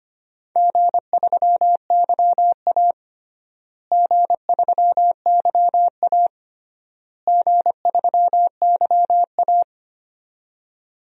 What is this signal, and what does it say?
Morse code, heard as a keyed tone, about 25 wpm, 700 Hz. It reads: G3YA G3YA G3YA
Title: DXCC-100 Call Signs - 4 Characters - Repeated 3x - 25wpm